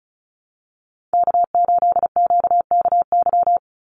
Morse code, heard as a keyed tone, about 34 wpm, 700 Hz. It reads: K8QKY
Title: Call Signs Repeated 2x - 35wpm